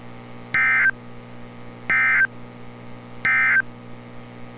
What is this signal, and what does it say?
Emergency Alert System (SAME) tones, heard as harsh buzzing bursts. It is heard from 0.4 to 3.7 s.
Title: NOAA Weather Radio - Altoona, PA